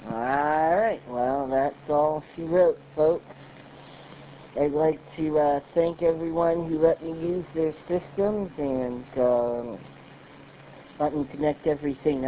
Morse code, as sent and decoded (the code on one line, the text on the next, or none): none